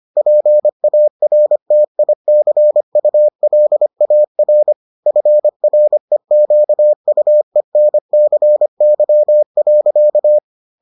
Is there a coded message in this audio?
Yes: Morse code from 0.2 to 10.4 s, heard as a keyed tone, about 25 wpm, 600 Hz. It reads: PARTICULAR FREQUENCY.